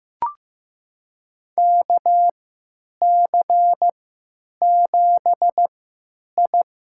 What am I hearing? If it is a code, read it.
Morse code, heard as a keyed tone, about 15 wpm, spaced as for 9 wpm, 700 Hz. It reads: KC7I
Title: USA Call Signs - 6 Characters - 3x Character Spacing - 15wpm